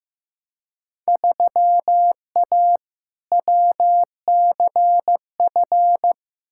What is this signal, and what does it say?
Morse code, heard as a keyed tone, about 15 wpm, 700 Hz. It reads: 3A WCF
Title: Field Day Exchanges Encore - 15wpm